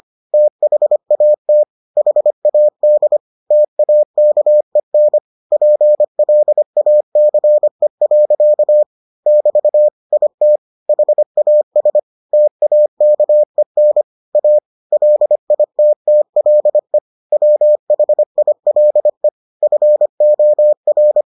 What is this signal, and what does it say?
Morse code, heard as a keyed tone, about 25 wpm, 600 Hz. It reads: THAT HAD TAKEN PLACE. <BT> IT HAS TAKEN A LITTLE WHILE FOR